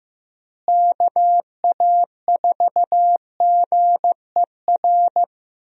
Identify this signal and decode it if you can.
Morse code, heard as a keyed tone, about 15 wpm, 700 Hz. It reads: KA4GER
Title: USA Call Signs - 6 Characters - Repeated 2x - 15wpm